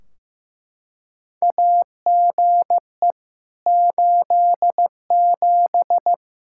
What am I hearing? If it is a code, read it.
Morse code, heard as a keyed tone, about 15 wpm, 700 Hz. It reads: AGE 87
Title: K1USN SST QSO Practice 15wpm